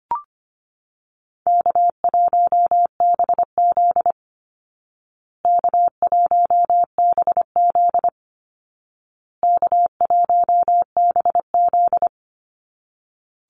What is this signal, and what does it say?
Morse code, heard as a keyed tone, about 25 wpm, 700 Hz. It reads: K1BZ K1BZ K1BZ
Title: Call Signs Repeated 2x - 25wpm